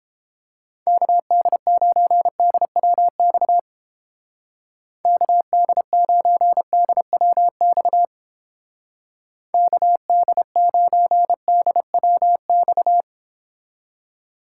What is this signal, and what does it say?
Morse code, heard as a keyed tone, about 29 wpm, 700 Hz. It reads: KD9DWX KD9DWX KD9DWX